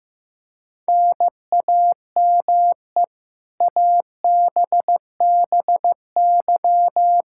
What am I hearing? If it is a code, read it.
Morse code, heard as a keyed tone, about 15 wpm, 700 Hz. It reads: NAME ABBY
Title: Practice QSO - Exchange 1 - Beginning - 15wpm